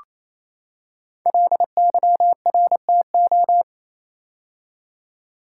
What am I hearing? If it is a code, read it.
Morse code, heard as a keyed tone, about 29 wpm, 700 Hz. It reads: LYRTO
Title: Code Groups - 5 Letters - 28wpm